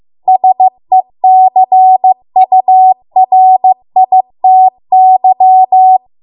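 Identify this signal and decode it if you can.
Morse code, heard as a keyed tone, about 15 wpm, 750 Hz. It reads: SECURITY
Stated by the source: Morse code practice at 15 wpm from the national amateur radio association's code archive.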